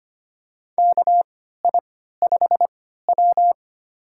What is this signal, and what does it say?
Morse code, heard as a keyed tone, about 25 wpm, spaced as for 14 wpm, 700 Hz. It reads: KI5W
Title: USA Call Signs - 4 Characters - 3x Character Spacing - 25wpm